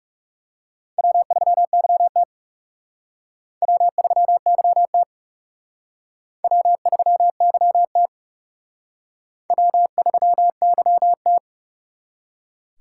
Morse code, heard as a keyed tone, about 37 wpm, 700 Hz. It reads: W3YT W3YT W3YT W3YT
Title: Speed Racing - USA Call Signs 30wpm